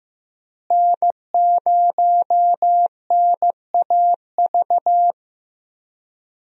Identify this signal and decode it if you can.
Morse code, heard as a keyed tone, about 15 wpm, 700 Hz. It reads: N0NAV
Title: USA Call Signs - 5 Characters - Repeated 3x - 15wpm